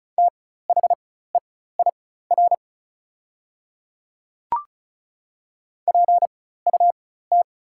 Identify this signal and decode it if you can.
Morse code, heard as a keyed tone, about 36 wpm, spaced as for 16 wpm, 700 Hz. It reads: THEIR PUT
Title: 4x Character Spacing - Top 200 Words - 35wpm